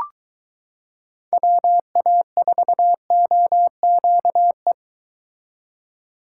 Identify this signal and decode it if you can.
Morse code, heard as a keyed tone, about 21 wpm, 700 Hz. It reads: WA4OQE